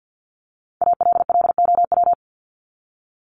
Morse code, heard as a keyed tone, about 50 wpm, 700 Hz. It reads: ALLOW
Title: ED Suffix - Top 100 Words with Base Words 50wpm